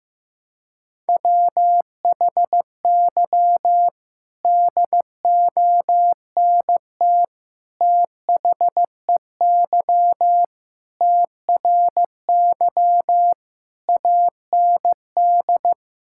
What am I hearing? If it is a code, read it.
Morse code, heard as a keyed tone, about 15 wpm, 700 Hz. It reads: WHY DONT THEY TRY AND
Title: Sentences from Top 1000 Words - Review Part 5 - 15wpm